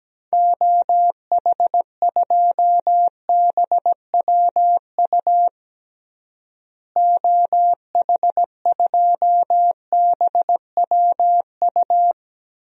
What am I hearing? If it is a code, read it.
Morse code, heard as a keyed tone, about 17 wpm, 700 Hz. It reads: OH2BWU OH2BWU
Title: DXCC-100 Call Signs - 6 Characters - Repeated 2x - 17wpm